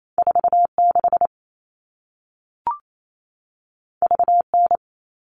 Morse code, heard as a keyed tone, about 28 wpm, 700 Hz. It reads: 46 VN
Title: All Cut Numbers - 2-Digit 28wpm